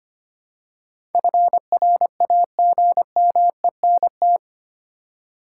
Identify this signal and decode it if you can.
Morse code, heard as a keyed tone, about 25 wpm, 700 Hz. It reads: FRAGMENT
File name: 8 Letter Words - First 3 Letters Given - 25wpm